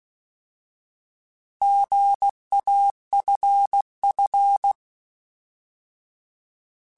Morse code, heard as a keyed tone, about 16 wpm, 800 Hz. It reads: GAFF